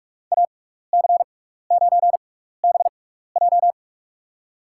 Morse code, heard as a keyed tone, about 41 wpm, spaced as for 15 wpm, 690 Hz. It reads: AC9BJ
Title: USA Call Signs - 5 Characters - 6x Character Spacing - 45wpm